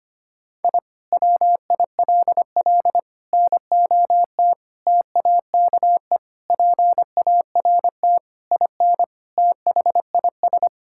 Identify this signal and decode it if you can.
Morse code, heard as a keyed tone, about 25 wpm, 700 Hz. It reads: I WILL NOT TAKE PART IN THIS
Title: Sentences from Top 1000 Words - Review Part 6 - 25wpm